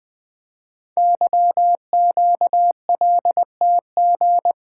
Morse code, heard as a keyed tone, about 20 wpm, 700 Hz. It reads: YQLTG
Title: Code Groups - 5 Letters - 20wpm